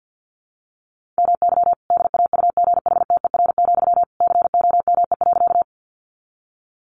Morse code, heard as a keyed tone, about 50 wpm, 700 Hz. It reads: MY DAUGHTER, COME.
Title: Sentences from Top 1500 Words 50wpm